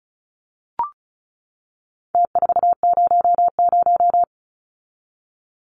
Morse code, heard as a keyed tone, about 34 wpm, 700 Hz. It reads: T400
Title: Numbers - Contest Style - 4-Digit 35wpm